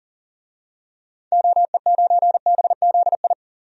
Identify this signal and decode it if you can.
Morse code, heard as a keyed tone, about 40 wpm, 700 Hz. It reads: OE9BZI
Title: DXCC-100 Call Signs - 6 Characters - 40wpm